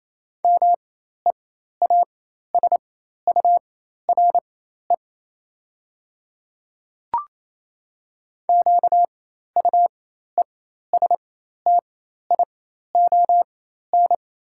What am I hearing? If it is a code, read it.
Morse code, heard as a keyed tone, about 28 wpm, spaced as for 13 wpm, 700 Hz. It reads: MEASURE QUESTION
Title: 4x Character Spacing - Top 500 Words - 28wpm